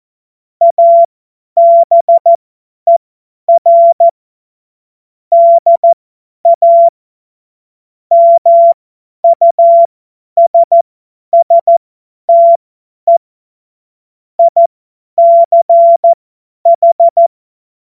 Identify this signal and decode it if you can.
Morse code, heard as a keyed tone, about 14 wpm, spaced as for 10 wpm, 680 Hz. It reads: ABER DA MUSSTE ICH